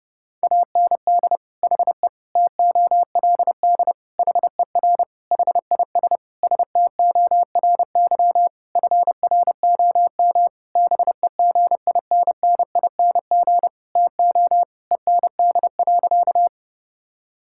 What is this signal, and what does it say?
Morse code, heard as a keyed tone, about 32 wpm, 700 Hz. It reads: AND HE TOLD HER HIS STORY FROM BEGINNING TO END.
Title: Sentences from Top 1000 Words - Review Part 5 - 30wpm